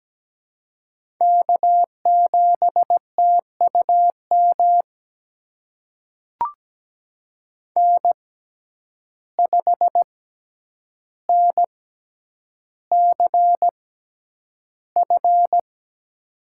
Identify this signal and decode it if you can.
Morse code, heard as a keyed tone, about 17 wpm, spaced as for 9 wpm, 700 Hz. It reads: K7TUM N5NCF
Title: USA Call Signs - 5 Characters - 6x Character Spacing - 17wpm